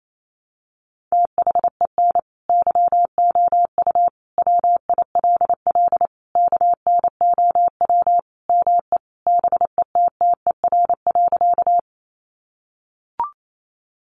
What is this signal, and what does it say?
Morse code, heard as a keyed tone, about 28 wpm, 700 Hz. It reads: THEN YOU WILL KNOW ME BETTER.
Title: Sentences from Top 400 Words 28wpm